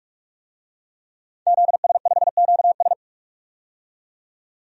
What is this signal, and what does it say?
Morse code, heard as a keyed tone, about 46 wpm, 700 Hz. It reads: ZS5QS
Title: DXCC-100 Call Signs - 5 Characters - 45wpm